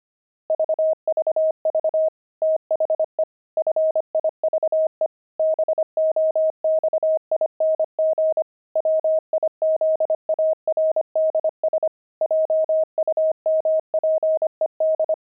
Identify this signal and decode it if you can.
Morse code, heard as a keyed tone, about 25 wpm, 630 Hz. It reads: VVV THE FIVE BOXING WIZARDS JUMPED
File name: pangram-fs-25-cs-25-jumping